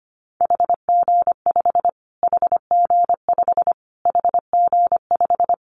Morse code, heard as a keyed tone, about 25 wpm, 700 Hz. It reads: HG5 HG5 HG5
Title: Callsign Prefixes for Top 25 DXCC - Repeated 3x - 25wpm